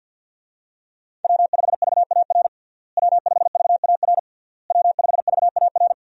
Morse code, heard as a keyed tone, about 52 wpm, 700 Hz. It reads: W5VAR W5VAR W5VAR
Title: USA Call Signs - 5 Characters - Repeated 3x - 50wpm